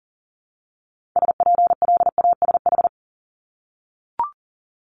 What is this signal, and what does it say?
Morse code, heard as a keyed tone, about 40 wpm, 700 Hz. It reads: SPLASH
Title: Multi-letter Phonemes Repeated 3x 40wpm